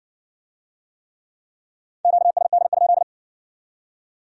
Morse code, heard as a keyed tone, about 58 wpm, 700 Hz. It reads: QSD?